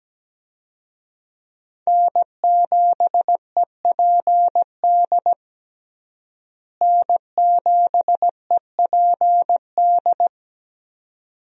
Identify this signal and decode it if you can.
Morse code, heard as a keyed tone, about 17 wpm, 700 Hz. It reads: N7EPD N7EPD
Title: USA Call Signs - 5 Characters - Repeated 2x - 17wpm